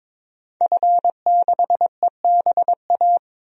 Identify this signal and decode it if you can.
Morse code, heard as a keyed tone, about 21 wpm, 700 Hz. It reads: F6EBA
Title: DXCC-100 Call Signs - 5 Characters - Repeated 3x - 22wpm